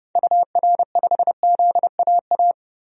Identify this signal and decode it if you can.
Morse code, heard as a keyed tone, about 32 wpm, 700 Hz. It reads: UR5ZAA
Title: DXCC-100 Call Signs - 6 Characters - 30wpm